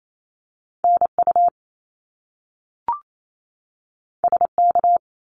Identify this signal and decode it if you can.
Morse code, heard as a keyed tone, about 28 wpm, 700 Hz. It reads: NU SK
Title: Canadian Provinces and Territories 28wpm